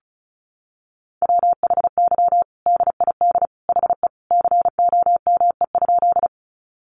Morse code, heard as a keyed tone, about 34 wpm, 700 Hz. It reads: WHY DID HE COME?
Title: Sentences from Top 200 Words 35wpm